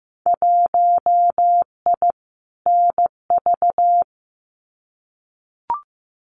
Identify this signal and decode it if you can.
Morse code, heard as a keyed tone, about 15 wpm, 700 Hz. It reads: 1I NV
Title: Winter Field Day Exchanges Encore 15wpm